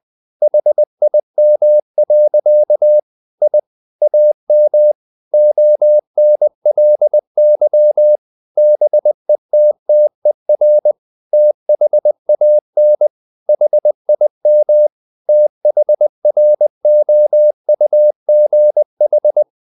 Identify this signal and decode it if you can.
Morse code, heard as a keyed tone, about 20 wpm, 600 Hz. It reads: HIM. I AM ONLY BETTER THAN HIM THROUGH
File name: the old man and the sea part 2 20wpm